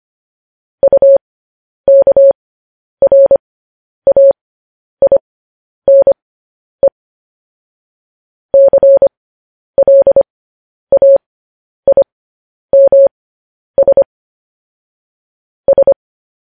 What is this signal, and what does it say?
Morse code, heard as a keyed tone, about 25 wpm, spaced as for 10 wpm, 550 Hz. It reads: UKRAINE CLAIMS S
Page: News Headlines in Morse Code at 10 WPM